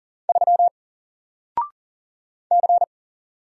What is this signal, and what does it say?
Morse code, heard as a keyed tone, about 40 wpm, 700 Hz. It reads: 3 C
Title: Single Letter-Number Alt 40wpm